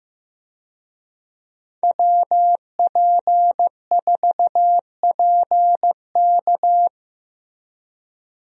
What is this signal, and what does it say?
Morse code, heard as a keyed tone, about 15 wpm, 700 Hz. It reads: WP4PK
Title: USA Call Signs - 5 Characters - Repeated 2x - 15wpm